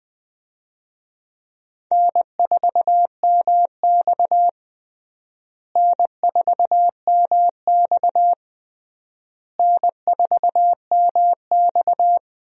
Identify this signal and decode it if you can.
Morse code, heard as a keyed tone, about 20 wpm, 700 Hz. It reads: N4MX N4MX N4MX
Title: USA Call Signs - 4 Characters - Repeated 3x - 20wpm